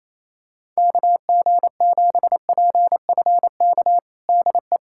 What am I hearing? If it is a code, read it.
Morse code, heard as a keyed tone, about 28 wpm, 700 Hz. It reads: KG7PFK DE
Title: USA Callsign DE Callsign - 6 Characters - 28wpm